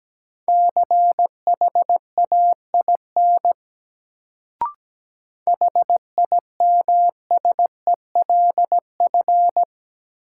Morse code, heard as a keyed tone, about 17 wpm, 700 Hz. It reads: CHAIN HIMSELF
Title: Sentences from Top 1500 Words - New Word Drill Alt - 17wpm